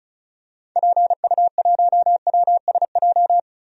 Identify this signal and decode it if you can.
Morse code, heard as a keyed tone, about 34 wpm, 700 Hz. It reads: PU1WSJ
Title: DXCC-100 Call Signs - 6 Characters - Repeated 3x - 35wpm